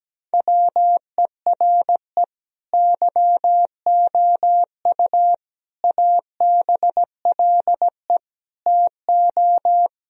Morse code, heard as a keyed tone, about 17 wpm, 700 Hz. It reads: WERE YOU ABLE TO
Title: Sentences from Top 1000 Words - Review Part 2 - 17wpm